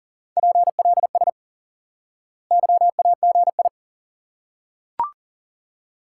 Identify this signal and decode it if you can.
Morse code, heard as a keyed tone, about 40 wpm, 700 Hz. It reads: PLS YAGI
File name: IQE Course 3 - Lesson 89 - YAGI - Two Elements - 40wpm